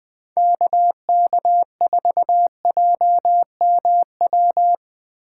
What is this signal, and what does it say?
Morse code, heard as a keyed tone, about 20 wpm, 700 Hz. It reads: KK4JMW
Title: USA Call Signs - 6 Characters - 3x Character Spacing - 20wpm